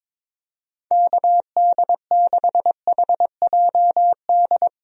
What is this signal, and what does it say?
Morse code, heard as a keyed tone, about 22 wpm, 700 Hz. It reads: KD6HJD
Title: USA Call Signs - 6 Characters - 22wpm